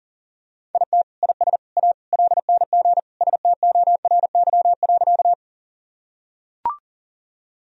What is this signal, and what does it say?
Morse code, heard as a keyed tone, about 37 wpm, 700 Hz. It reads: IT IS A LNG STORY.
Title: Sentences from Top 300 Words and Abbreviations 40wpm